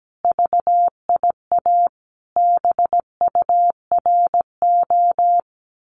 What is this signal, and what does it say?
Morse code, heard as a keyed tone, about 17 wpm, 700 Hz. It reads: VIA BURO
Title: Common QSO Phrases 17wpm